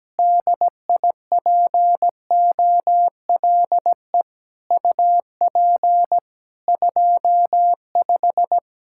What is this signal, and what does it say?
Morse code, heard as a keyed tone, about 17 wpm, 700 Hz. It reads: DIPOLE UP 25
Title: Practice QSO - Exchange 2 - Info - 17wpm